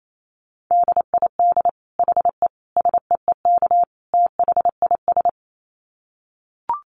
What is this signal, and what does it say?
Morse code, heard as a keyed tone, about 28 wpm, 700 Hz. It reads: DID HE SEEK THIS
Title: Dit Challenge 28wpm